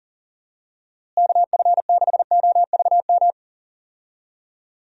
Morse code, heard as a keyed tone, about 40 wpm, 700 Hz. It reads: KF6OVM
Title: USA Call Signs - 6 Characters - Repeated 2x - 40wpm